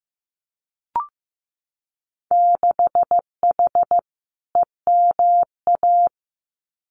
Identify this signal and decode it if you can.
Morse code, heard as a keyed tone, about 15 wpm, 700 Hz. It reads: 6H EMA
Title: Winter Field Day Exchanges Encore 15wpm